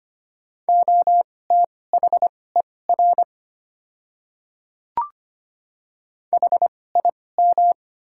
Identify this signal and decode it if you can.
Morse code, heard as a keyed tone, about 25 wpm, spaced as for 18 wpm, 690 Hz. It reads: OTHER HIM
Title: 2x Character Spacing - Top 100 Words - 25wpm